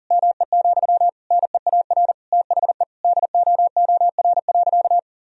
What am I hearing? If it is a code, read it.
Morse code, heard as a keyed tone, about 40 wpm, 700 Hz. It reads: ME, NEAR THE DOOR.
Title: Sentences from Top 1000 Words - Review Part 2 - 40wpm